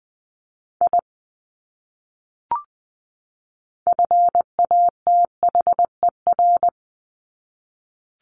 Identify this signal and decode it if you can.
Morse code, heard as a keyed tone, about 19 wpm, 700 Hz. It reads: I FATHER
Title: Top 200 Words 20wpm